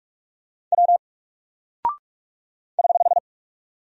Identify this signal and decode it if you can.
Morse code, heard as a keyed tone, about 44 wpm, 700 Hz. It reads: W <HH>
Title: Single Character (Letter:Number:Punctuation:Prosign) Alt 45wpm